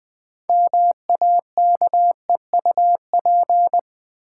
Morse code, heard as a keyed tone, about 20 wpm, 700 Hz. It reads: MAKEUP
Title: Build A Word - 3 Letters At A Time - 6-Letter Words - 20wpm